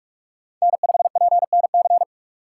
Morse code, heard as a keyed tone, about 46 wpm, 700 Hz. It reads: N5PNC
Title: USA Call Signs - 3x Character Spacing - 45wpm